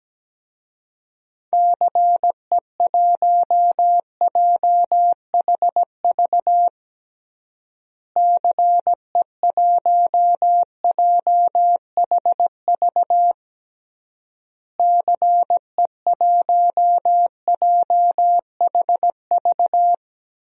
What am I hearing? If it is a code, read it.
Morse code, heard as a keyed tone, about 17 wpm, 700 Hz. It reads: CE1JHV CE1JHV CE1JHV
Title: DXCC-100 Call Signs - 6 Characters - Repeated 3x - 17wpm